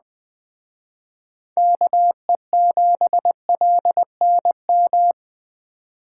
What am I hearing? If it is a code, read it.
Morse code, heard as a keyed tone, about 20 wpm, 700 Hz. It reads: KE7LNM
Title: Speed Racing - USA 2x3 Call Signs 17wpm